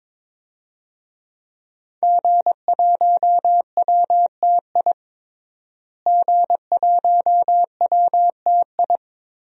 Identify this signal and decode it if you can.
Morse code, heard as a keyed tone, about 22 wpm, 700 Hz. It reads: G1WTI G1WTI
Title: DXCC-100 Call Signs - 5 Characters - Repeated 2x - 22wpm